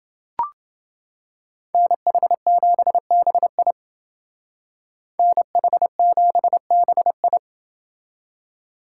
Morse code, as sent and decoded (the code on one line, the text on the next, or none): -. .... --... -... .. / -. .... --... -... ..
NH7BI NH7BI